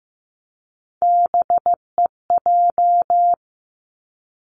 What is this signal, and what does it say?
Morse code, heard as a keyed tone, about 15 wpm, 700 Hz. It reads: BEJ
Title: Three Challenging Letter-Number Combinations 15wpm